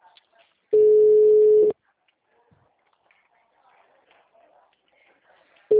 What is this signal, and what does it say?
Ringback tone (Europe, 425 Hz), with speech.
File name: out- -2017- - - .70855